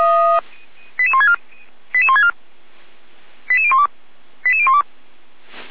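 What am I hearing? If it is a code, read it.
Five-tone selective call (ZVEI), heard as quick runs of five tones, tones of about 70 ms. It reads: 88164 88164 88012 88012
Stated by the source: ZVEI five-tone call received on SDR, from audio.f85360.feuerwehr-orlamuende.org, Orlamünde fire brigade.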